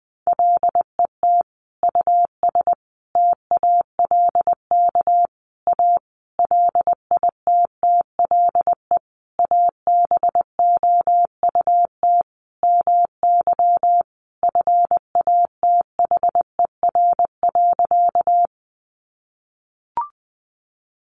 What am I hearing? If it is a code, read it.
Morse code, heard as a keyed tone, about 20 wpm, 700 Hz. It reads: LET US TALK A LITTLE ABOUT MY FATHER.
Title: Sentences from Top 400 Words 20wpm